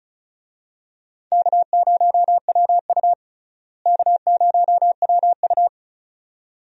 Morse code, heard as a keyed tone, about 36 wpm, 700 Hz. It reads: K0WU K0WU
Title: USA Call Signs - 4 Characters - Repeated 2x - 35wpm